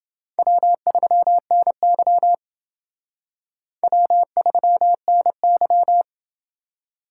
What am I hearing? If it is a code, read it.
Morse code, heard as a keyed tone, about 28 wpm, 700 Hz. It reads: W3NY W3NY